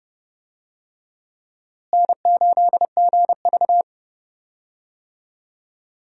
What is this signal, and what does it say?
Morse code, heard as a keyed tone, about 32 wpm, 700 Hz. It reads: N8GV
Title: USA Call Signs - 4 Characters - Repeated 2x - 30wpm